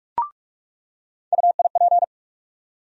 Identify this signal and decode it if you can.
Morse code, heard as a keyed tone, about 41 wpm, 700 Hz. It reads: UIP
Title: USA Call Sign Postfixes - 3 Letters 45wpm